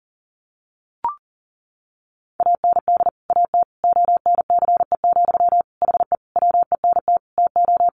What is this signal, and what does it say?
Morse code, heard as a keyed tone, about 40 wpm, 700 Hz. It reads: AND AT ONCE, HE WENT TO
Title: Sentences from Top 1300 Words 40wpm